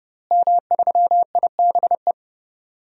Morse code, heard as a keyed tone, about 30 wpm, 700 Hz. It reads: M3IBE
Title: DXCC-100 Call Signs - 5 Characters - Repeated 3x - 30wpm